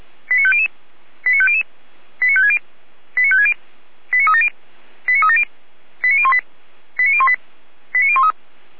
Five-tone selective call (ZVEI), heard as quick runs of five tones, tones of about 70 ms. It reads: 89500 89500 89560 89560 89370 89370 89018 89018 89013